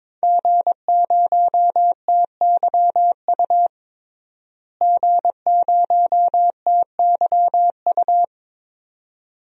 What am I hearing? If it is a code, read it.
Morse code, heard as a keyed tone, about 22 wpm, 700 Hz. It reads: G0TYU G0TYU
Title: DXCC-100 Call Signs - 5 Characters - Repeated 2x - 22wpm